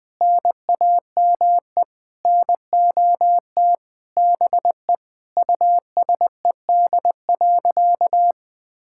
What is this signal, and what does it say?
Morse code, heard as a keyed tone, about 20 wpm, 700 Hz. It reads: NAME NOT BE USED.
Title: Sentences with 9-Word Phrases - 20wpm